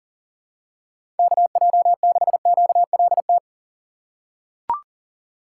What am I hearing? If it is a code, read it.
Morse code, heard as a keyed tone, about 40 wpm, 700 Hz. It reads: KJ6QLT
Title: USA Call Signs - 6 Characters - 4x Character Spacing - 40wpm